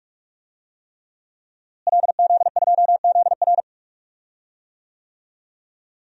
Morse code, heard as a keyed tone, about 42 wpm, 700 Hz. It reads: LZ2ZR
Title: DXCC-100 Call Signs - 5 Characters - Repeated 3x - 45wpm